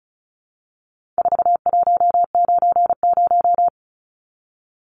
Morse code, heard as a keyed tone, about 34 wpm, 700 Hz. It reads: 4190
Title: Numbers - 4-Digit 35wpm